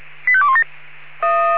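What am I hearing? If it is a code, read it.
Five-tone selective call (ZVEI), heard as quick runs of five tones, tones of about 70 ms. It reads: 85217
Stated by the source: Five-tone ZVEI selective call received on an SDR.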